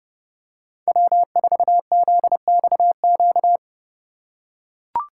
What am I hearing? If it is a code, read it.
Morse code, heard as a keyed tone, about 32 wpm, 700 Hz. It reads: W4ZXQ
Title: USA Call Signs - 5 Characters - 3x Character Spacing - 30wpm